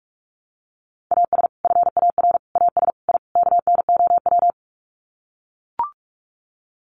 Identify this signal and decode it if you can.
Morse code, heard as a keyed tone, about 44 wpm, 700 Hz. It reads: AS FAR AS I KNOW
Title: Sets of 5 Words 45wpm